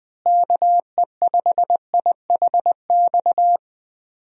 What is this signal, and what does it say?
Morse code, heard as a keyed tone, about 20 wpm, 700 Hz. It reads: KE5IHX
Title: USA Call Signs - 6 Characters - Repeated 3x - 20wpm